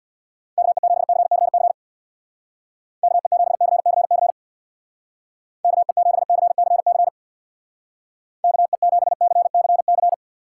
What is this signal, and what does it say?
Morse code, heard as a keyed tone, about 63 wpm, 700 Hz. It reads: KE7CCC KE7CCC KE7CCC KE7CCC